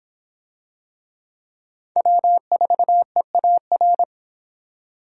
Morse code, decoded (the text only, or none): W4EAR